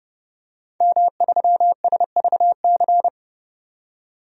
Morse code, heard as a keyed tone, about 32 wpm, 700 Hz. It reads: M3SVC
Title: DXCC-100 Call Signs - 5 Characters - 30wpm